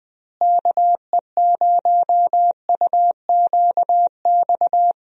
Morse code, heard as a keyed tone, about 20 wpm, 700 Hz. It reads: KE0UQX
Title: USA Call Signs - 6 Characters - Repeated 3x - 20wpm